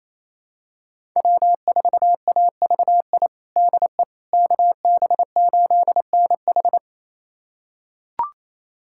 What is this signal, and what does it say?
Morse code, heard as a keyed tone, about 28 wpm, 700 Hz. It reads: W4AVI DE KB8NH
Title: USA Callsign DE Callsign - 5 Characters - 28wpm